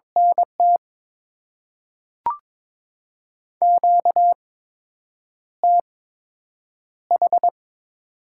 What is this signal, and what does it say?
Morse code, heard as a keyed tone, about 22 wpm, 700 Hz. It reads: N T Q T H
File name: Top 20 QSO Elements - 8x Character Spacing - 22wpm